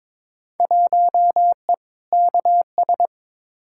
Morse code, heard as a keyed tone, about 22 wpm, 700 Hz. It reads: 1E KS
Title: Field Day Exchanges - 22wpm